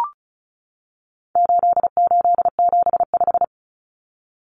Morse code, heard as a keyed tone, about 34 wpm, 700 Hz. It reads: 8875
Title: Common Cut Numbers - 4-Digit 35wpm